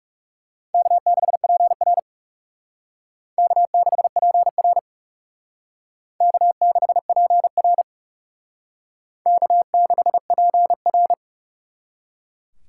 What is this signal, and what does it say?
Morse code, heard as a keyed tone, about 37 wpm, 700 Hz. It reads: K6PR K6PR K6PR K6PR
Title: Speed Racing - USA 1x2 Call Signs 30wpm